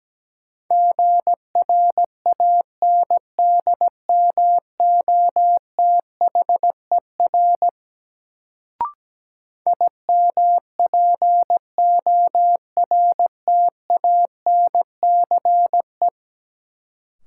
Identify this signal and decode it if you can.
Morse code, heard as a keyed tone, about 17 wpm, 700 Hz. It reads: GRANDMOTHER IMPORTANCE
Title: Sentences from Top 1200 Words - New Word Drill - 17wpm